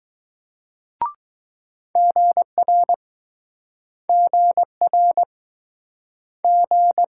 Morse code, heard as a keyed tone, about 22 wpm, 700 Hz. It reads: GR GR G